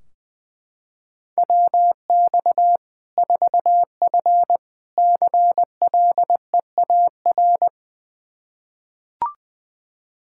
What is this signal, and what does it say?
Morse code, heard as a keyed tone, about 21 wpm, 700 Hz. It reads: WX 4F CLEAR
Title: Practice QSO - Exchange 2 - Info - 20wpm